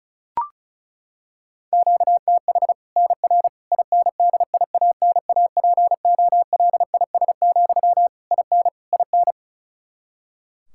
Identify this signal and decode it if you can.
Morse code, heard as a keyed tone, about 36 wpm, 700 Hz. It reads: QTH NR INDIANAPOLIS, IN IN
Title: Practice QSO - Exchange 1 - Beginning - 35wpm